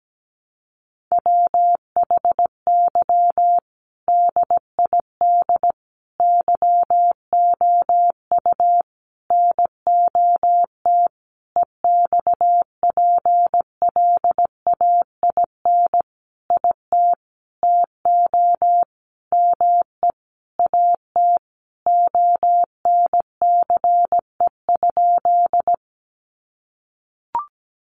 Morse code, heard as a keyed tone, about 17 wpm, 700 Hz. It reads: WHY DID YOU NOT EXPLAIN IT TO ME AT ONCE?
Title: Sentences from Top 1100 Words 17wpm